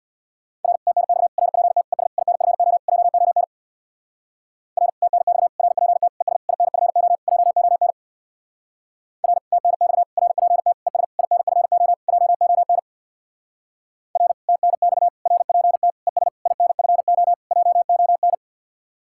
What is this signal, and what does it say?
Morse code, heard as a keyed tone, about 60 wpm, 700 Hz. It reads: R TNX RPT ES INFO JON R TNX RPT ES INFO JON R TNX RPT ES INFO JON R TNX RPT ES INFO JON
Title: Speed-Racing Common QSO Phrases - 50wpm